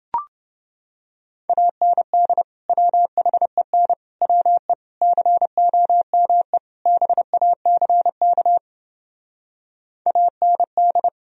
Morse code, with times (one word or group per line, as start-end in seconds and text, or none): 1.5-2.4 AND
2.7-3.9 WHEN
4.2-4.7 WE
5.0-6.6 COME
6.9-8.6 BACK
10.1-11.1 AND